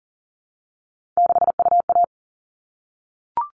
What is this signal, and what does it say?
Morse code, heard as a keyed tone, about 40 wpm, 700 Hz. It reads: 6UA
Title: All Cut Numbers - 3-Digit 40wpm